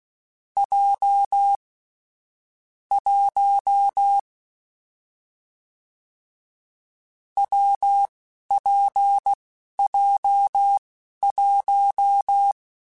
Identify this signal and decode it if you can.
Morse code, heard as a keyed tone, about 16 wpm, spaced as for 11 wpm, 800 Hz. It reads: J 1 WPJ1
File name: Difficult Combos / Difficult Combos-mp3s / WPJ WPJ1 @10